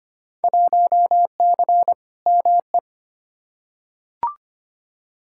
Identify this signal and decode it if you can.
Morse code, heard as a keyed tone, about 25 wpm, 700 Hz. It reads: 1C ME
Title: Field Day Exchanges - 25wpm